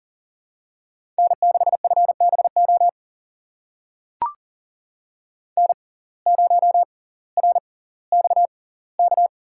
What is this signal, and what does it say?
Morse code, heard as a keyed tone, about 38 wpm, spaced as for 20 wpm, 700 Hz. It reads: N6FBO N0RXK
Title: USA Call Signs - 5 Characters - 6x Character Spacing - 40wpm